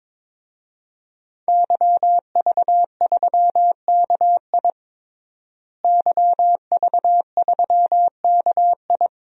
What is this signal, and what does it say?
Morse code, heard as a keyed tone, about 21 wpm, 700 Hz. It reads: YV3KI YV3KI
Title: DXCC-100 Call Signs - 5 Characters - Repeated 2x - 22wpm